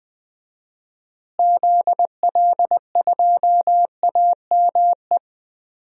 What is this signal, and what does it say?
Morse code, heard as a keyed tone, about 20 wpm, 690 Hz. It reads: ZL2AME
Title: DXCC-100 Call Signs - 6 Characters - Repeated 2x - 20wpm